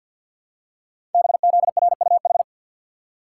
Morse code, heard as a keyed tone, about 48 wpm, 690 Hz. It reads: BZRUH
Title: Code Groups - 5 Letters - 50wpm